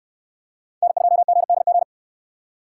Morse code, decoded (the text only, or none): N2GDG